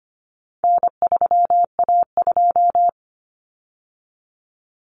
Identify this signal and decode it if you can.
Morse code, heard as a keyed tone, about 25 wpm, 700 Hz. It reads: N3A2
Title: Common Cut Numbers - 4-Digit 25wpm